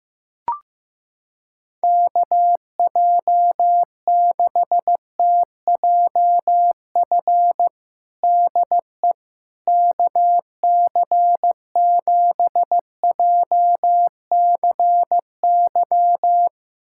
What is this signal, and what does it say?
Morse code, heard as a keyed tone, about 16 wpm, 700 Hz. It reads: KJ6TJF DE KC7JCY
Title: USA Callsign DE Callsign - 6 Characters - 15wpm